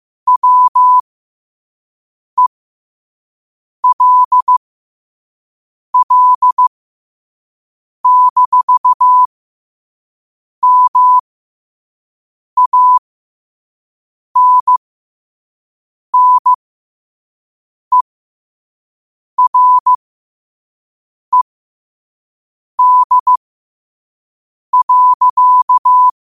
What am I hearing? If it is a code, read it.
Morse code, heard as a keyed tone, about 15 wpm, spaced as for 5 wpm, 1000 Hz. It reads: WELL-MANNERED.
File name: qotdfiles / 5wpm